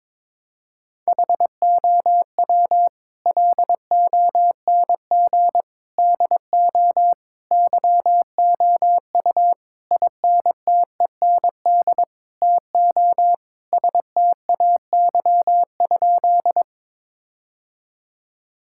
Morse code, heard as a keyed tone, about 22 wpm, 700 Hz. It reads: HOW LONG DO YOU INTEND TO STAY?
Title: Sentences from Top 1600 Words 22wpm